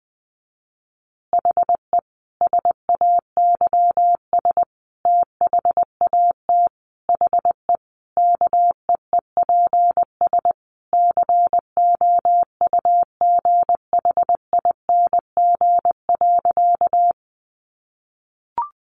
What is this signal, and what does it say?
Morse code, heard as a keyed tone, about 20 wpm, 700 Hz. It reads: HE SAYS THAT HE KEEPS COUGHING.
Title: Additional Easy Phrases 20wpm